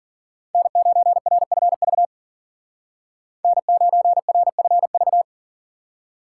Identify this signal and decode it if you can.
Morse code, heard as a keyed tone, about 52 wpm, 700 Hz. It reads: N9RFV N9RFV